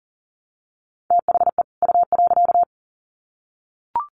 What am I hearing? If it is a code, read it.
Morse code, heard as a keyed tone, about 40 wpm, 700 Hz. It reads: THE U.
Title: Sets of 2 Words 40wpm